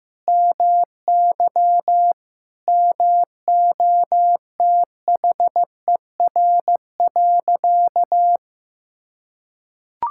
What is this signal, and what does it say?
Morse code, heard as a keyed tone, about 16 wpm, 700 Hz. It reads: MY MOTHER.
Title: Sentences from Top 1000 Words - Review Part 6 - 15wpm